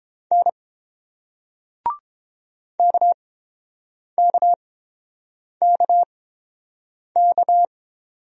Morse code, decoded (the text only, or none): N KKKK